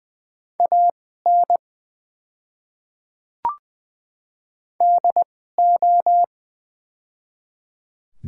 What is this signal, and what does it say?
Morse code, heard as a keyed tone, about 20 wpm, spaced as for 15 wpm, 700 Hz. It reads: AN DO
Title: 2x Character Spacing - Top 100 Words - 20wpm